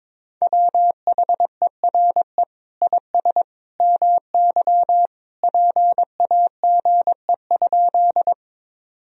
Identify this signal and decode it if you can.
Morse code, heard as a keyed tone, about 21 wpm, 700 Hz. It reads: WHERE IS MY PAGE?
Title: Sentences from Top 1000 Words - Review Part 1 - 22wpm